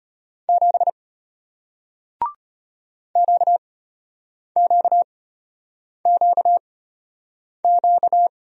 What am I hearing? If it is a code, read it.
Morse code, heard as a keyed tone, about 33 wpm, spaced as for 8 wpm, 690 Hz. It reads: 7 QQQQ